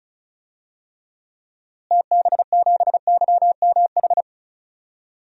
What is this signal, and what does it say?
Morse code, heard as a keyed tone, about 36 wpm, 700 Hz. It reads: TB7YMH
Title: DXCC-100 Call Signs - 6 Characters - Repeated 3x - 35wpm